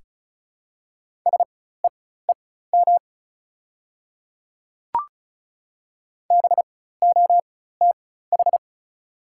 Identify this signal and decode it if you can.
Morse code, heard as a keyed tone, about 36 wpm, spaced as for 16 wpm, 700 Hz. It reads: SEEM BOTH
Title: 4x Character Spacing - Top 300 Words - 35wpm